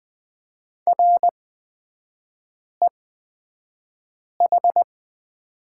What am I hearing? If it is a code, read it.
Morse code, heard as a keyed tone, about 20 wpm, spaced as for 5 wpm, 700 Hz. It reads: REH
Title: 255 Lesson 41e - Period - Three Characters 20wpm